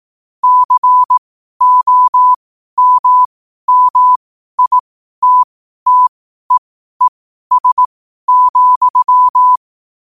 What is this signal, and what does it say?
Morse code, heard as a keyed tone, about 18 wpm, spaced as for 12 wpm, 1000 Hz. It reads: COMMITTEES,
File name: qotdfiles / 12wpm